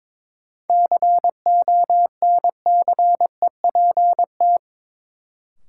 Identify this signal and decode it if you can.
Morse code, heard as a keyed tone, about 22 wpm, 700 Hz. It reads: CONCEPT
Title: Sentences from Top 1100 Words - New Word Drill - 22wpm